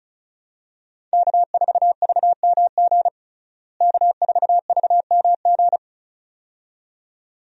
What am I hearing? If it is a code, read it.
Morse code, heard as a keyed tone, about 36 wpm, 700 Hz. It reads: K4VMG K4VMG
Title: USA Call Signs - 5 Characters - Repeated 2x - 35wpm